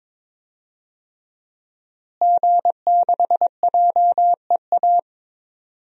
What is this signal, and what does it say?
Morse code, heard as a keyed tone, about 22 wpm, 700 Hz. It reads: G6JEA
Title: DXCC-100 Call Signs - 5 Characters - Repeated 3x - 22wpm